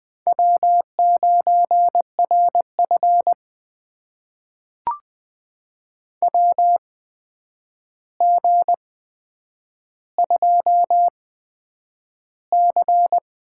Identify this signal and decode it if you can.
Morse code, heard as a keyed tone, about 20 wpm, spaced as for 9 wpm, 700 Hz. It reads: W9RF WG2C
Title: USA Call Signs - 4 Characters - 8x Character Spacing - 20wpm